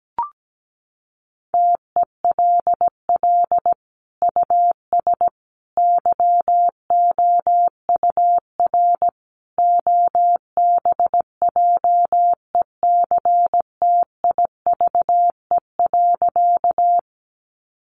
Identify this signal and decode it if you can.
Morse code, heard as a keyed tone, about 17 wpm, 700 Hz. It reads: TELL US YOUR OBJECTIVE.
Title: Sentences from Top 1200 Words 17wpm